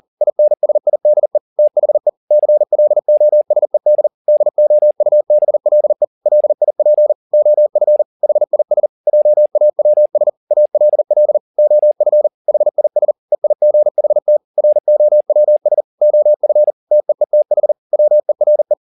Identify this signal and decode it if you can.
Morse code, heard as a keyed tone, about 40 wpm, 600 Hz. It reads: INSIDE THE CLOSED DOUBLE LIP OF HIS JAWS ALL OF HIS EIGHT ROWS OF TEETH WERE
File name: the old man and the sea 2 40wpm